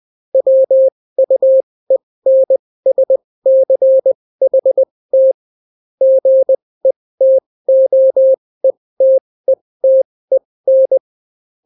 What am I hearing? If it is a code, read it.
Morse code, heard as a keyed tone, about 20 wpm, spaced as for 16 wpm, 530 Hz. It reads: WUENSCHT GETOETETEN